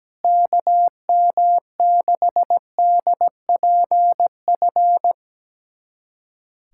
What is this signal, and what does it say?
Morse code, heard as a keyed tone, about 17 wpm, 700 Hz. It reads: KM6DPF
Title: USA Callsign DE Callsign - 6 Characters - 17wpm